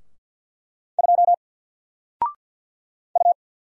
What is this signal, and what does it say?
Morse code, heard as a keyed tone, about 50 wpm, 700 Hz. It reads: 2 U
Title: Single Letter-Number Alt 50wpm